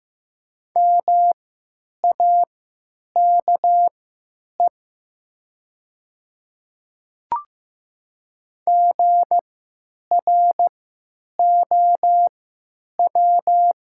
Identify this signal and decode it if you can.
Morse code, heard as a keyed tone, about 15 wpm, spaced as for 8 wpm, 700 Hz. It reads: MAKE GROW
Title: 3x Character Spacing - Top 500 Words - 15wpm